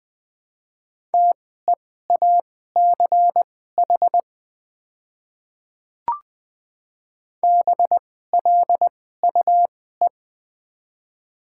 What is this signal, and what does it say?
Morse code, heard as a keyed tone, about 20 wpm, spaced as for 14 wpm, 700 Hz. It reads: TEACH BLUE
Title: 2x Character Spacing - Top 500 Words - 20wpm